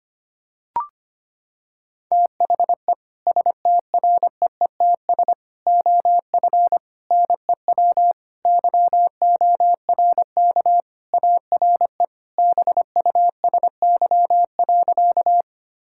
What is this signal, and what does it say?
Morse code, heard as a keyed tone, about 25 wpm, 700 Hz. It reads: THE STREETS OF NEW YORK ARE BUSY.
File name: practice / 25wpm / Sentences with 3-Word Phrases - 25wpm